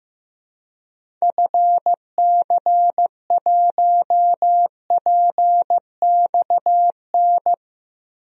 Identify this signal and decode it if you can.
Morse code, heard as a keyed tone, about 15 wpm, 700 Hz. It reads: FC1PXN
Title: DXCC-100 Call Signs - 6 Characters - 15wpm